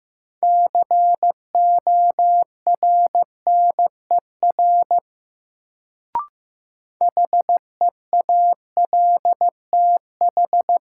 Morse code, heard as a keyed tone, about 15 wpm, 700 Hz. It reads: CORNER HEALTH
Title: Sets of 5 Words - New Word Drill - 15wpm